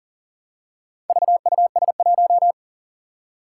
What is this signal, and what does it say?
Morse code, heard as a keyed tone, about 38 wpm, 700 Hz. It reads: VUS1
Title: Four Challenging Letter-Number Combinations 40wpm